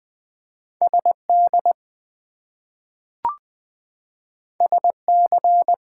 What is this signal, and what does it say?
Morse code, heard as a keyed tone, about 20 wpm, 700 Hz. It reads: SD SC
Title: US States and Canadian Provinces 20wpm